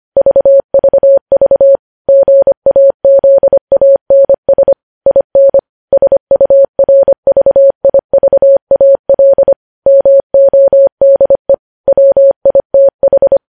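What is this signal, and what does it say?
Morse code, heard as a keyed tone, about 25 wpm, 550 Hz. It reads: VVV GAZANS IN SURVIVAL MODE WITH
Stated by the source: News headlines sent in Morse code at 25 wpm.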